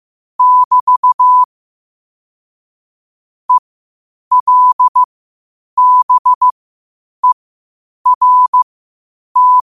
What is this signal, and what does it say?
Morse code, heard as a keyed tone, about 15 wpm, spaced as for 8 wpm, 1000 Hz. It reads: <BT> ELBERT
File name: qotdfiles / 8wpm